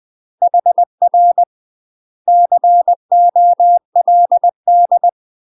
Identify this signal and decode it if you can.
Morse code, heard as a keyed tone, about 20 wpm, 700 Hz. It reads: HR COLD
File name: UK MASTER QSOs TRAINING / 20WPM / UK MASTER QSOs 0020